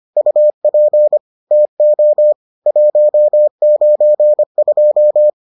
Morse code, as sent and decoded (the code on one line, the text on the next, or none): ..- .--. / - --- / .---- ----. ..---
UP TO 192